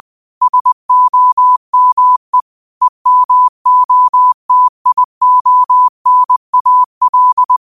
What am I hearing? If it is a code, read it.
Morse code, heard as a keyed tone, about 20 wpm, 1000 Hz. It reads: SOME EMOTIONAL